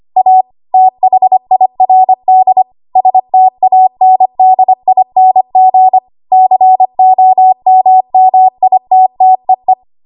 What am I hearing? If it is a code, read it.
Morse code, heard as a keyed tone, about 25 wpm, 750 Hz. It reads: A THIRD STANDING COMMITTEE
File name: Archive / 25 WPM / 25WPM